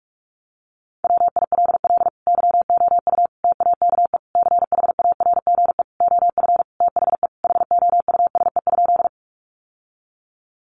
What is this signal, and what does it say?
Morse code, heard as a keyed tone, about 44 wpm, 700 Hz. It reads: WILL YOU TAKE CHARGE OF THE HOUSE?
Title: Sentences from Top 900 Words 45wpm